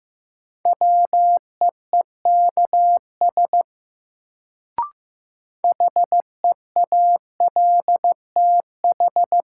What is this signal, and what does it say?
Morse code, heard as a keyed tone, about 15 wpm, 700 Hz. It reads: WEEKS HEALTH
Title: Sets of 5 Words - New Word Drill - 15wpm